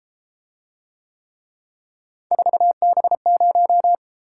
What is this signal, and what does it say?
Morse code, heard as a keyed tone, about 32 wpm, 700 Hz. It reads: 4B0